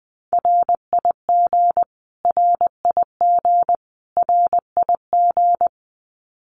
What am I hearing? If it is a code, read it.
Morse code, heard as a keyed tone, about 19 wpm, 700 Hz. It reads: RIG RIG RIG
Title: Abbreviations Repeated 3x - 20wpm